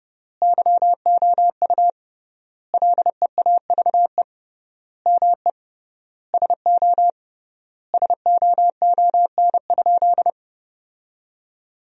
Morse code, heard as a keyed tone, about 32 wpm, 700 Hz. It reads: YOU LEAVE ME SO SOON?
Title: 3x Word-Spacing - Sentences from Top 400 Words - 30wpm